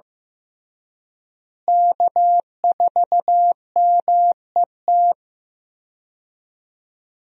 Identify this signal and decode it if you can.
Morse code, heard as a keyed tone, about 15 wpm, 700 Hz. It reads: K4MET
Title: USA Call Signs - 5 Characters - Repeated 2x - 15wpm